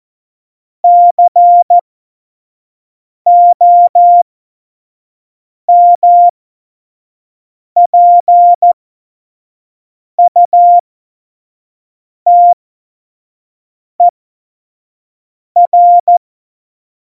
Morse code, heard as a keyed tone, about 14 wpm, spaced as for 5 wpm, 700 Hz. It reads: COMPUTER
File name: Morsekurs / Witze / Witz 15. Januar 2019, 25 F70